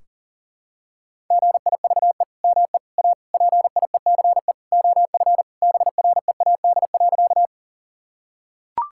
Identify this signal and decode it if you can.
Morse code, heard as a keyed tone, about 40 wpm, 700 Hz. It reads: GIVE ME A PIECE OF BREAD.
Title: Sentences from Top 1600 Words 40wpm